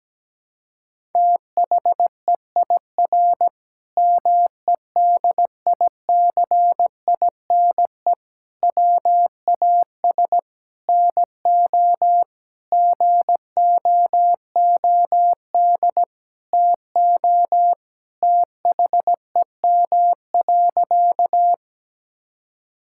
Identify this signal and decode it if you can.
Morse code, heard as a keyed tone, about 17 wpm, 700 Hz. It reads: THEIR MEDICINE WAS NO GOOD TO THEM.
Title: Sentences from Top 1000 Words - Review Part 6 - 17wpm